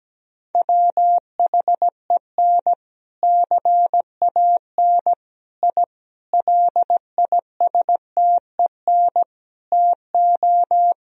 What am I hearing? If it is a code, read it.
Morse code, heard as a keyed tone, about 17 wpm, 700 Hz. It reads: WHEN CAN I LISTEN TO
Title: Sentences from Top 1000 Words - Review Part 3 - 17wpm